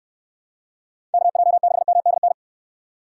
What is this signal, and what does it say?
Morse code, heard as a keyed tone, about 69 wpm, 700 Hz. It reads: KJ6MDN